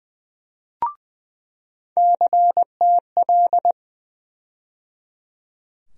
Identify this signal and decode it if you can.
Morse code, heard as a keyed tone, about 20 wpm, 700 Hz. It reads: CTL
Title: USA Call Sign Postfixes - 3 Letters 20wpm